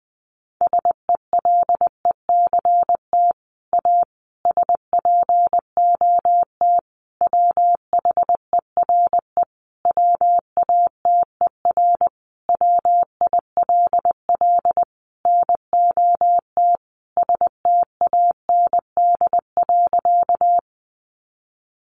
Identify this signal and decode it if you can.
Morse code, heard as a keyed tone, about 20 wpm, 700 Hz. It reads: SELECT A SPOT WHERE WATER WILL NOT STAND.
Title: Sentences from Top 1300 Words 20wpm